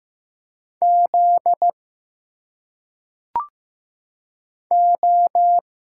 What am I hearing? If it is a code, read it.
Morse code, heard as a keyed tone, about 15 wpm, 700 Hz. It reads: Z O
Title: Single Letters - Phonetic Pronunciation 15wpm